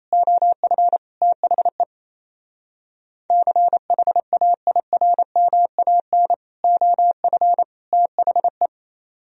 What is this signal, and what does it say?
Morse code, heard as a keyed tone, about 30 wpm, 700 Hz. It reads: OF THE CHAIRMAN OF THE
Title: Speed-Racing - Sets of 3 Words 28wpm